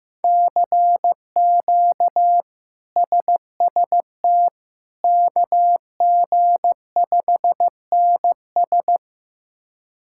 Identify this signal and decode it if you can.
Morse code, heard as a keyed tone, about 15 wpm, 700 Hz. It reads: CQ SST KG5NS
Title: K1USN SST QSO Practice - 15wpm